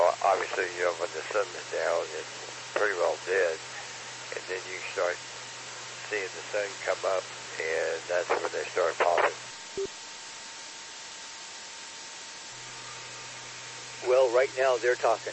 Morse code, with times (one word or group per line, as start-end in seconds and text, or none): none